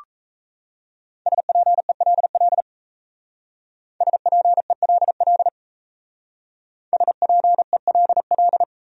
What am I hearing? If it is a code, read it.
Morse code, heard as a keyed tone, about 32 wpm, 700 Hz. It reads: SPELL SPELL SPELL
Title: Speed-Racing Top 300 Words 28wpm